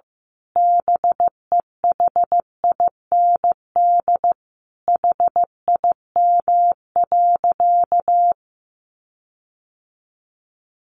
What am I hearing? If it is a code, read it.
Morse code, heard as a keyed tone, about 15 wpm, 700 Hz. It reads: BEHIND HIM.
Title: Sentences from Top 1400 Words 15wpm